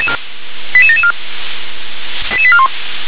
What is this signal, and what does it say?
Five-tone selective call (ZVEI), heard as quick runs of five tones, tones of about 70 ms. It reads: 88774 88841
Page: SDR/ZVEI